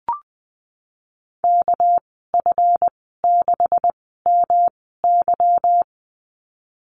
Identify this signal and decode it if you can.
Morse code, heard as a keyed tone, about 20 wpm, spaced as for 15 wpm, 700 Hz. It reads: KF6MY
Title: 207 Lesson 33g - Letter j - Call Signs 20wpm